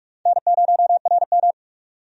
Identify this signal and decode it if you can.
Morse code, heard as a keyed tone, about 46 wpm, 700 Hz. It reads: N0RM
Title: USA Call Signs - 4 Characters - 45wpm